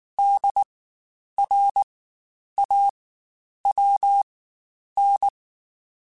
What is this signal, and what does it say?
Morse code, heard as a keyed tone, about 19 wpm, spaced as for 9 wpm, 800 Hz. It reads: DRAWN
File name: Spaced Alphas / Spaced Alphas mp3s / Ds sp @18